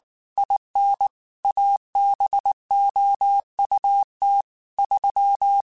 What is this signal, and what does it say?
Morse code, heard as a keyed tone, about 19 wpm, 800 Hz. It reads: IN ABOUT 3
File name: QSOs / QSOs-mp3s / WD0BGP 1@18